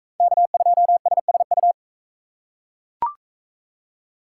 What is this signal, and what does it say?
Morse code, heard as a keyed tone, about 41 wpm, 700 Hz. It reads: K2SSU